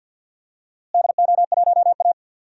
Morse code, heard as a keyed tone, about 47 wpm, 700 Hz. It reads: DO1A
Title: DXCC-100 Call Signs - 4 Characters - Repeated 3x - 50wpm